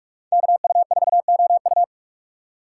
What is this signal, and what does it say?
Morse code, heard as a keyed tone, about 43 wpm, 700 Hz. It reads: KU4OU